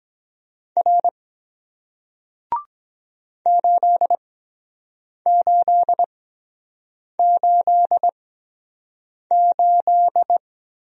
Morse code, heard as a keyed tone, about 24 wpm, spaced as for 7 wpm, 700 Hz. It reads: R 8888